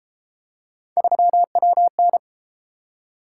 Morse code, heard as a keyed tone, about 30 wpm, 700 Hz. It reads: 3WN